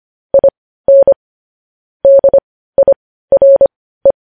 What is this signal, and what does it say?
Morse code, heard as a keyed tone, about 25 wpm, spaced as for 15 wpm, 550 Hz. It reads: IN DIRE
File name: podcast / News headlines-1x-25x15